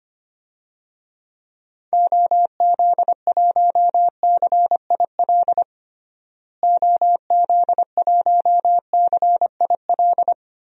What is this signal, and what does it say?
Morse code, heard as a keyed tone, about 25 wpm, 700 Hz. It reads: OZ1CIL OZ1CIL
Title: DXCC-100 Call Signs - 6 Characters - Repeated 2x - 25wpm